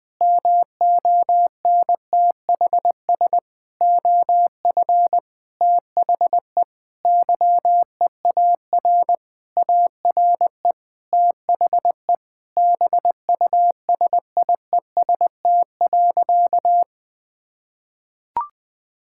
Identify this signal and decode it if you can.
Morse code, heard as a keyed tone, about 20 wpm, 700 Hz. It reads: MONTHS OF THE YEAR ARE THE BUSIEST.
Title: Sentences with 7-Word Phrases - 20wpm